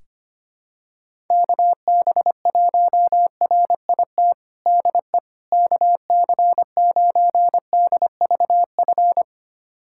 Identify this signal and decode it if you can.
Morse code, heard as a keyed tone, about 25 wpm, 700 Hz. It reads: KB1RIT DE KC9DVF
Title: USA Callsign DE Callsign - 6 Characters - 25wpm